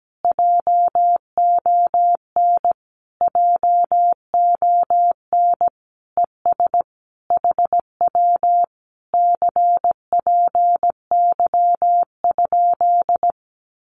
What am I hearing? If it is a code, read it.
Morse code, heard as a keyed tone, about 17 wpm, 700 Hz. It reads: JON JON ES HW CPY?
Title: Common QSO Phrases Encore 17wpm